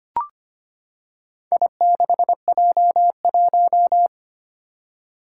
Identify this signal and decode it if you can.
Morse code, heard as a keyed tone, about 25 wpm, 700 Hz. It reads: I6J1
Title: Four Challenging Letter-Number Combinations 25wpm